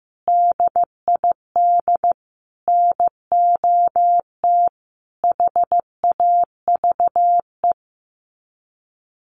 Morse code, heard as a keyed tone, about 15 wpm, 700 Hz. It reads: DID NOT HAVE
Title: Sets of 4 Words 15wpm